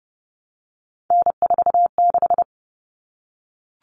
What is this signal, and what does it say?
Morse code, heard as a keyed tone, about 30 wpm, 700 Hz. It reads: N46
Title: All Cut Numbers - 3-Digit 30wpm